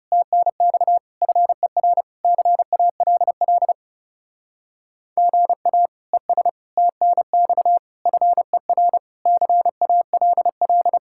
Morse code, heard as a keyed tone, about 37 wpm, 700 Hz. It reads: TNX FER CALL GA ES TNX FER CALL
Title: Speed-Racing Common QSO Phrases - 30wpm